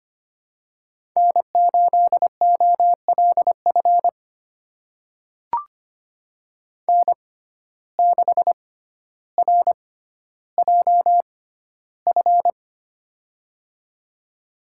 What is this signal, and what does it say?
Morse code, heard as a keyed tone, about 25 wpm, spaced as for 13 wpm, 700 Hz. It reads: N8OLF N6RJF
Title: USA Call Signs - 5 Characters - 6x Character Spacing - 25wpm